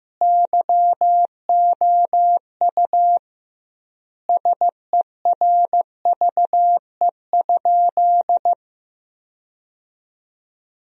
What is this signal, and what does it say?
Morse code, heard as a keyed tone, about 16 wpm, 700 Hz. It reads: YOU SERVE?
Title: 2x Word-Spacing - Sentences from Top 400 Words - 15wpm